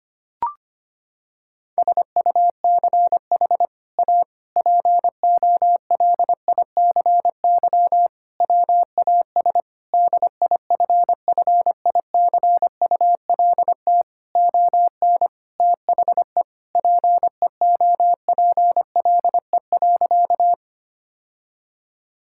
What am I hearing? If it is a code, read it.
Morse code, heard as a keyed tone, about 25 wpm, 700 Hz. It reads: SUCH A POLICY WAS DIFFICULT ON THE PEOPLE.
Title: Sentences from Top 1000 Words - Review Part 5 - 25wpm